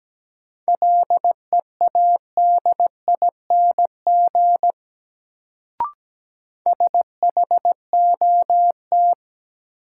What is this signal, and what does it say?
Morse code, heard as a keyed tone, about 17 wpm, 700 Hz. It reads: LEADING SHOT
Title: Sentences from Top 1100 Words - New Word Drill Alt - 17wpm